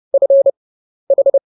Morse code, heard as a keyed tone, about 30 wpm, 550 Hz. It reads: F H